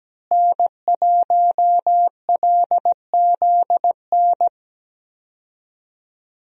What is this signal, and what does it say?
Morse code, heard as a keyed tone, about 17 wpm, 700 Hz. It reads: N1LZN
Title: USA Call Signs - 5 Characters - Repeated 3x - 17wpm